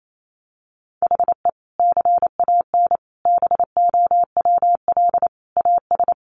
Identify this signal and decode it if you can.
Morse code, heard as a keyed tone, about 28 wpm, 700 Hz. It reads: HE CAN BOWL AS
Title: Sentences from Top 1500 Words 28wpm